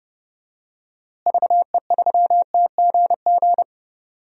Morse code, heard as a keyed tone, about 30 wpm, 700 Hz. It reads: VE3TGG
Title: DXCC-100 Call Signs - 6 Characters - Repeated 3x - 30wpm